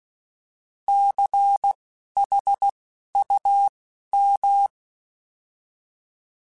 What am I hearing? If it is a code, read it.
Morse code, heard as a keyed tone, about 16 wpm, spaced as for 11 wpm, 800 Hz. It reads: CHUM
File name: Alphabetical / Alphabetical mp3s / Cs @10